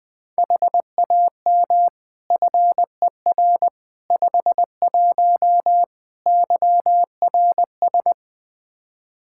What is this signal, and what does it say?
Morse code, heard as a keyed tone, about 21 wpm, 700 Hz. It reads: HAM FER 51 YRS
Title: Practice QSO - Exchange 3 - More Info - 20wpm